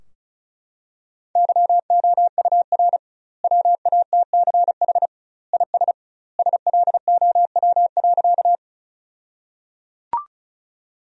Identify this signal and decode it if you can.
Morse code, heard as a keyed tone, about 34 wpm, 700 Hz. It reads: YOUR WATCH IS SLOW.
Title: 2x Word-Spacing - Sentences from Top 400 Words - 35wpm